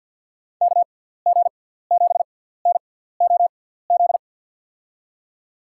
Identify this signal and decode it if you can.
Morse code, heard as a keyed tone, about 52 wpm, spaced as for 17 wpm, 700 Hz. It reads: KG7NOZ
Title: USA Call Signs - 6 Characters - 6x Character Spacing - 50wpm